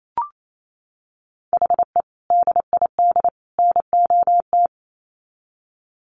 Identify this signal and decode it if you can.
Morse code, heard as a keyed tone, about 28 wpm, 700 Hz. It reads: HE DID NOT
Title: Sets of 3 Words 28wpm